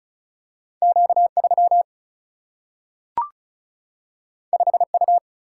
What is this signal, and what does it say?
Morse code, heard as a keyed tone, about 36 wpm, 700 Hz. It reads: Q3 5U
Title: Single-Letter Single-Number 35wpm